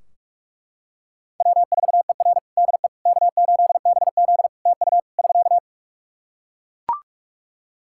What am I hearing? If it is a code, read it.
Morse code, heard as a keyed tone, about 42 wpm, 700 Hz. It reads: W4ER DE K8BZ TU <SK>
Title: Practice QSO - Exchange 4 - Ending - 45wpm